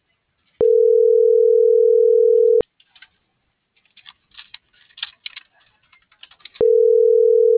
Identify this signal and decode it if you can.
Ringback tone (North America), 440 + 480 Hz.